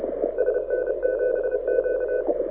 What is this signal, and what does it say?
Morse code, heard as a keyed tone, about 33 wpm, 1500 Hz, with radio noise. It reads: SN7Q
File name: audio zawody / IARU 2015 CW SN7Q